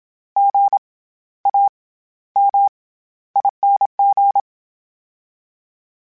Morse code, heard as a keyed tone, about 27 wpm, 800 Hz. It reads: G A M ING
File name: Special INGs-mp3s / Special INGs-mp3s / SlowToFaster ing @25